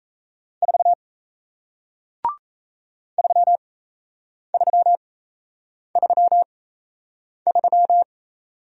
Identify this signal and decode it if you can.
Morse code, heard as a keyed tone, about 38 wpm, spaced as for 8 wpm, 700 Hz. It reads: 4 3333